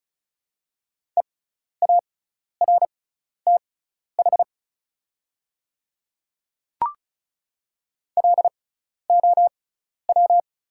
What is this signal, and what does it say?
Morse code, heard as a keyed tone, about 36 wpm, spaced as for 12 wpm, 700 Hz. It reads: EARTH LOW
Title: 6x Character Spacing - Top 200 Words - 35wpm